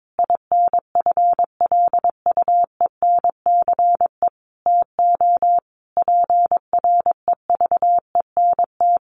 Morse code, heard as a keyed tone, about 22 wpm, 700 Hz. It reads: INFLUENCE TO PREVENT
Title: Sentences from Top 1100 Words 22wpm